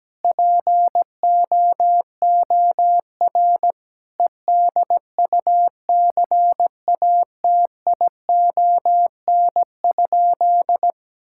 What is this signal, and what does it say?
Morse code, heard as a keyed tone, about 17 wpm, 700 Hz. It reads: POOR EDUCATION?
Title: Sentences from Top 1000 Words - Review Part 6 - 17wpm